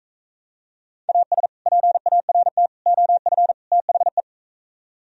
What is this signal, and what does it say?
Morse code, heard as a keyed tone, about 43 wpm, 700 Hz. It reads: AS PART OF THE